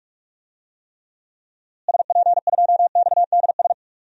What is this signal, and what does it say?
Morse code, heard as a keyed tone, about 46 wpm, 700 Hz. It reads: SP2XDS
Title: DXCC-100 Call Signs - 6 Characters - Repeated 2x - 45wpm